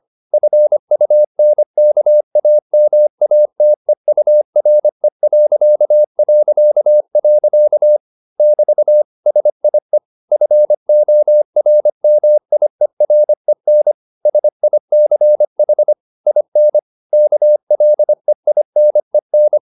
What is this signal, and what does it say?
Morse code, heard as a keyed tone, about 25 wpm, 600 Hz. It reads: FUNKAMATEURE... <BT> SIE FORMIEREN SICH IN KLEINEN